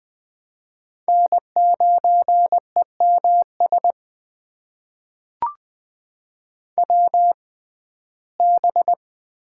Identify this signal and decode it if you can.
Morse code, heard as a keyed tone, about 21 wpm, spaced as for 15 wpm, 700 Hz. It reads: N9EMS WB
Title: USA Call Signs - 6x Character Spacing - 20wpm